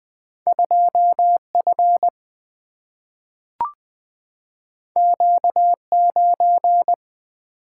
Morse code, heard as a keyed tone, about 20 wpm, 700 Hz. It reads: 2 F Q 9
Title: Single-Letter Single-Number 20wpm